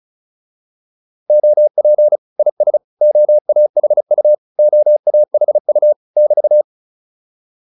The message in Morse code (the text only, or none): OP IS OAHU OAHU <BT>